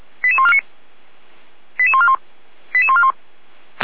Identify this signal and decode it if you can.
Five-tone selective call (ZVEI), heard as quick runs of five tones, tones of about 70 ms. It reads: 88149 88141 88141